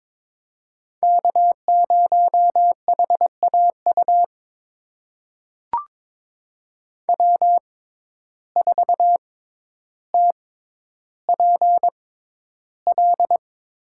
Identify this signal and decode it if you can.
Morse code, heard as a keyed tone, about 22 wpm, spaced as for 11 wpm, 700 Hz. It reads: K0HAU W4TPL
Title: USA Call Signs - 5 Characters - 6x Character Spacing - 22wpm